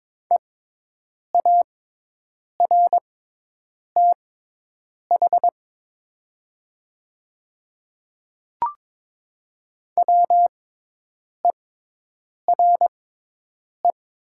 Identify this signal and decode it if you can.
Morse code, heard as a keyed tone, about 22 wpm, spaced as for 8 wpm, 700 Hz. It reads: EARTH WERE
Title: 6x Character Spacing - Top 500 Words - 22wpm